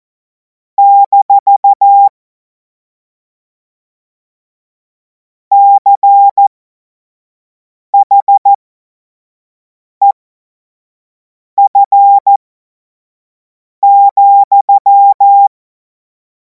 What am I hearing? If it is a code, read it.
Morse code, heard as a keyed tone, about 14 wpm, spaced as for 5 wpm, 800 Hz. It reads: - CHEF,
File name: Morsekurs / Witze / Witz 10. Februar 2026 25 F70